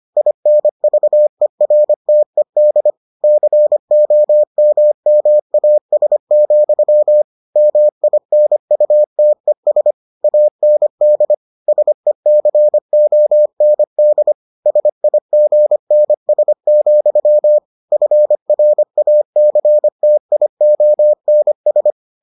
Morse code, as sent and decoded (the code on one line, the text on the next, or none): .. -. ...- . .-. - . -.. / -.-. --- -- -- .- ... --..-- / -- .. -. ..- - . ... / .- -. -.. / ... -.-. --- -. -.. / ... .. --. -. ... --..-- / ..-. .-. .- -.-. - .. --- -. ...
INVERTED COMMAS, MINUTES AND SECOND SIGNS, FRACTIONS